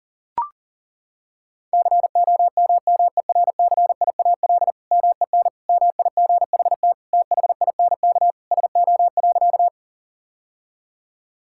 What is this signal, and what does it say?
Morse code, heard as a keyed tone, about 40 wpm, 700 Hz. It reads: COMMERCIAL MEN MIGHT THINK SO.
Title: Sentences from Top 1000 Words - Review Part 5 - 40wpm